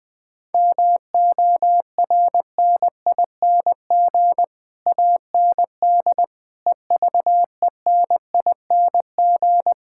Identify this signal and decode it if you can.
Morse code, heard as a keyed tone, about 20 wpm, 700 Hz. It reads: MORNING AND EVENING
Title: Sets of 9 Words - 20wpm